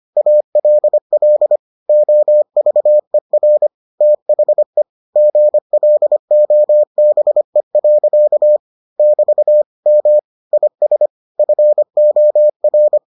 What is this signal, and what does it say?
Morse code, heard as a keyed tone, about 25 wpm, 600 Hz. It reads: ALL OVER THE GLOBE. <BT> M IS FOR